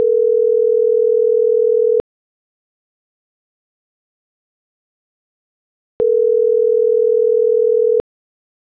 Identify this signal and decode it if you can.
Ringback tone (North America), 440 + 480 Hz.